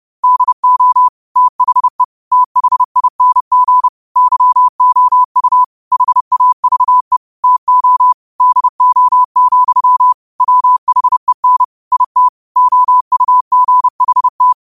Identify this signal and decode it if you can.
Morse code, heard as a keyed tone, about 30 wpm, 1000 Hz. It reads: DO THE THING YOU HAVE TO DO, WHEN IT OUGHT
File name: qotdfiles / 30wpm